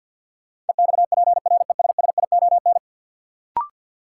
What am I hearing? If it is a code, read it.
Morse code, heard as a keyed tone, about 53 wpm, 700 Hz. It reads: EXPRESSION
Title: Sentences from Top 1200 Words - New Word Drill Alt - 50wpm